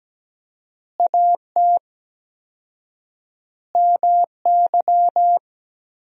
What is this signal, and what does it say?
Morse code, heard as a keyed tone, about 17 wpm, 700 Hz. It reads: AT MY
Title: 4x Word-Spacing - Sentences from Top 200 Words - 17wpm